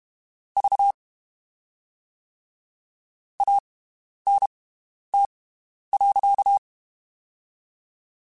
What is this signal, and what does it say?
Morse code, heard as a keyed tone, about 32 wpm, 800 Hz. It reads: V A N T .
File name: (spaced) / QSOs Spaced-mp3s / WD0FBT 1 @30 sp